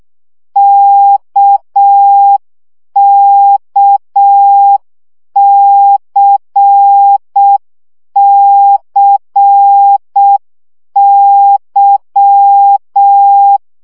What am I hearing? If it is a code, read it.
Morse code, heard as a keyed tone, about 6 wpm, 800 Hz. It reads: KKCCY
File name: CW gyakorlas k c y 5 baud